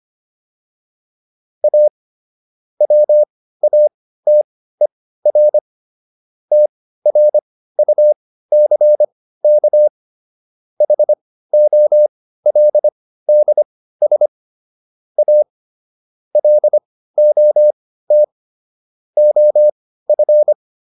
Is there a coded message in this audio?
Yes: Morse code from 1.6 to 20.5 s, heard as a keyed tone, about 25 wpm, spaced as for 15 wpm, 600 Hz. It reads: A WATER TRUCK HOLDS A LOT OF